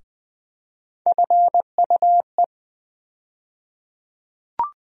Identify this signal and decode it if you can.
Morse code, heard as a keyed tone, about 20 wpm, 700 Hz. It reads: FUE
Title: USA Call Sign Postfixes - 3 Letters 20wpm